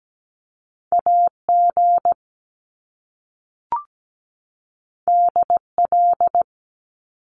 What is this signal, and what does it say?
Morse code, heard as a keyed tone, about 17 wpm, 700 Hz. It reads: AG DL
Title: Top Two Letter Combinations in Words 17wpm